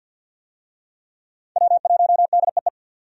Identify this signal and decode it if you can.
Morse code, heard as a keyed tone, about 52 wpm, 700 Hz. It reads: W1DEE
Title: USA Call Signs - 5 Characters - 50wpm